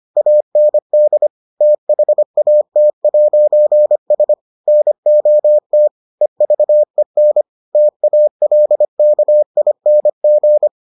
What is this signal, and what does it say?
Morse code, heard as a keyed tone, about 25 wpm, 600 Hz. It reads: AND THAT'S NOT EVEN TALKING